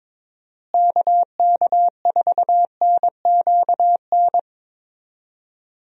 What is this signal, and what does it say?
Morse code, heard as a keyed tone, about 22 wpm, 700 Hz. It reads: KK4NQN
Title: USA Call Signs - 6 Characters - Repeated 3x - 22wpm